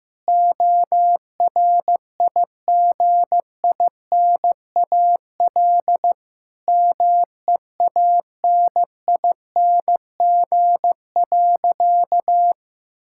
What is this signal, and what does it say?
Morse code, heard as a keyed tone, about 15 wpm, 700 Hz. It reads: ORIGINAL MEANING.
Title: Sentences from Top 1000 Words - Review Part 6 - 15wpm